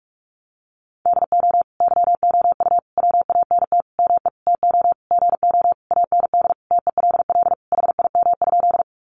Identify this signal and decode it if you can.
Morse code, heard as a keyed tone, about 44 wpm, 700 Hz. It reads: DO YOU WANT ME TO GO AND TELL HIM?
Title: Sentences from Top 200 Words 45wpm